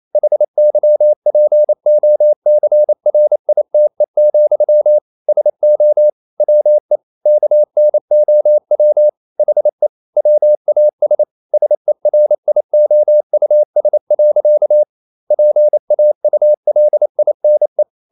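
Morse code, heard as a keyed tone, about 28 wpm, 600 Hz. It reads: HYPOCRITE, SO WE KNOW HE WAS SERIOUS. PAULINE
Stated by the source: Morse code practice at 28 wpm, text of Ken Follett's novel Never in English, from CW med Gnister.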